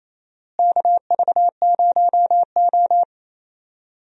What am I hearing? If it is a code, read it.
Morse code, heard as a keyed tone, about 28 wpm, 700 Hz. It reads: KV0O
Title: USA Call Signs - 4 Characters - Repeated 2x - 28wpm